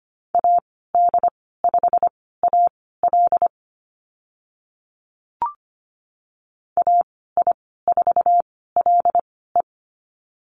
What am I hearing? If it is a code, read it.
Morse code, heard as a keyed tone, about 25 wpm, spaced as for 16 wpm, 690 Hz. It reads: AD5AL AI4LE
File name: Lesson 15g - Number 5 - Call Signs 25wpm